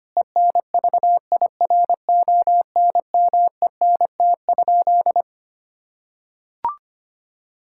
Morse code, heard as a keyed tone, about 25 wpm, 700 Hz. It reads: ENVIRONMENT?
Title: Sentences from Top 1000 Words - Review Part 5 - 25wpm